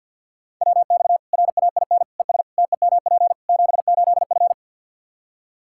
Morse code, heard as a keyed tone, about 46 wpm, 700 Hz. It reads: WX RAIN ES TEMP 78F
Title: K1USN SST QSO Practice 50wpm